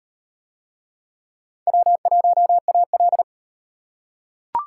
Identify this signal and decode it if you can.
Morse code, heard as a keyed tone, about 37 wpm, 700 Hz. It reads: W1AL